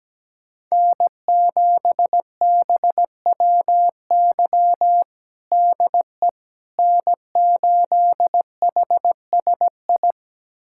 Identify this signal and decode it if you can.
Morse code, heard as a keyed tone, about 17 wpm, 700 Hz. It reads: N7BWY DE N8HSI
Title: USA Callsign DE Callsign - 5 Characters - 17wpm